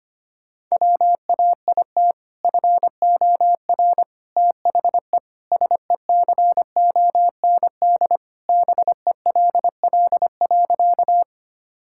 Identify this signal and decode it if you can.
Morse code, heard as a keyed tone, about 25 wpm, 700 Hz. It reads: WAIT FOR THE SECOND BELL.
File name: practice / 25wpm / Sentences with 2-Word Phrases - 25wpm